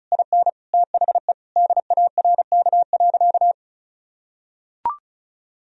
Morse code, heard as a keyed tone, about 34 wpm, 700 Hz. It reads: IN THE DARK.
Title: Sentences from Top 1000 Words - Review Part 4 - 35wpm